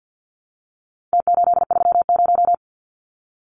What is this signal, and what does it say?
Morse code, heard as a keyed tone, about 50 wpm, 700 Hz. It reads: T830
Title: Numbers - Contest Style - 4-Digit 50wpm